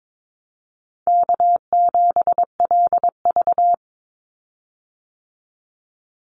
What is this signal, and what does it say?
Morse code, heard as a keyed tone, about 22 wpm, 700 Hz. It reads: K7LV
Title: Call Signs - 22wpm